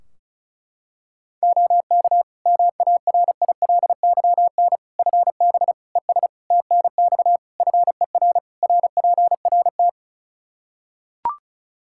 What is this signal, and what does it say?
Morse code, heard as a keyed tone, about 34 wpm, 700 Hz. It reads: OK MARILYN FB ES TNX FER RPRT
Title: Practice QSO - Exchange 2 - Info - 35wpm